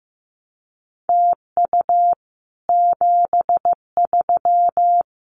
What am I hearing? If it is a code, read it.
Morse code, heard as a keyed tone, about 15 wpm, 700 Hz. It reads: TU 73
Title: Common QSO Phrases Encore 15wpm